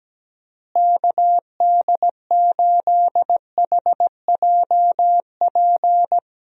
Morse code, heard as a keyed tone, about 17 wpm, 700 Hz. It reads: KD8HJP
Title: USA Call Signs - 6 Characters - Repeated 2x - 17wpm